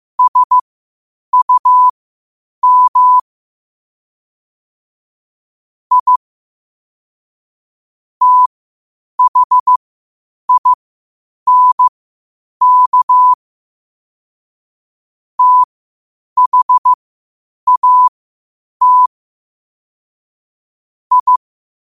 Morse code, heard as a keyed tone, about 15 wpm, spaced as for 8 wpm, 1000 Hz. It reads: SUM I THINK THAT I
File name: qotdfiles / 8wpm